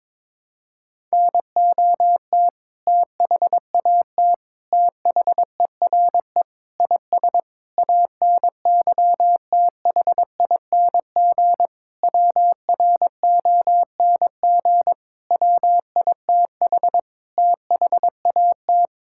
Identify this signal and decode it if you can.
Morse code, heard as a keyed tone, about 22 wpm, 700 Hz. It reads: NOT THAT THERE IS ANYTHING WRONG WITH THAT
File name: Sentences with 7-Word Phrases - 22wpm